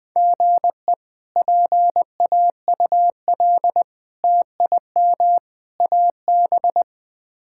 Morse code, heard as a keyed tone, about 21 wpm, 700 Hz. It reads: GE PAUL TIM AB
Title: K1USN SST QSO Practice - 20wpm